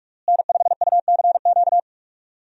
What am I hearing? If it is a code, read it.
Morse code, heard as a keyed tone, about 41 wpm, 700 Hz. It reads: N5UCQ
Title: USA Call Signs - 5 Characters - Repeated 3x - 45wpm